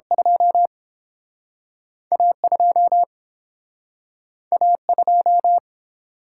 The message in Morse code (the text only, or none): 2 A 2 A 2